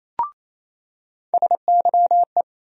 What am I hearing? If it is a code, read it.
Morse code, heard as a keyed tone, about 28 wpm, 700 Hz. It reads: SYE
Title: USA Call Sign Postfixes - 3 Letters 28wpm